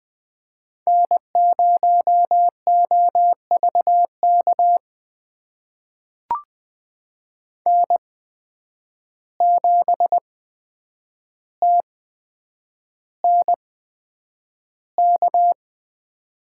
Morse code, heard as a keyed tone, about 20 wpm, spaced as for 9 wpm, 700 Hz. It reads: N0OVK N7TNK
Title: USA Call Signs - 5 Characters - 8x Character Spacing - 20wpm